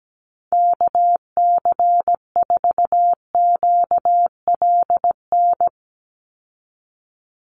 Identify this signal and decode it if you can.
Morse code, heard as a keyed tone, about 17 wpm, 700 Hz. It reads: KC4QLN
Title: USA Call Signs 17wpm